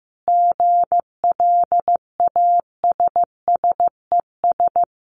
Morse code, heard as a keyed tone, about 15 wpm, 700 Hz. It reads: GLASSES
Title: Additional Easy Phrases 15wpm